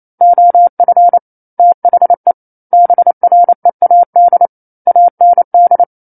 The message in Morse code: OF THE BREAD AND